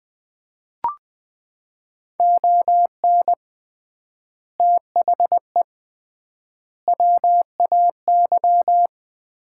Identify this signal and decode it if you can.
Morse code, heard as a keyed tone, about 21 wpm, 700 Hz. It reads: ON THE WAY
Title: 3x Word-Spacing - Sets of 4 Words - 20wpm